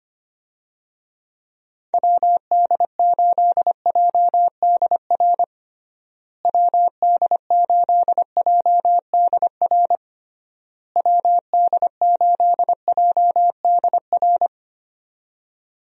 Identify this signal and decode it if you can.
Morse code, heard as a keyed tone, about 25 wpm, 700 Hz. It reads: WD8JDR WD8JDR WD8JDR
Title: USA Call Signs - 6 Characters - Repeated 3x - 25wpm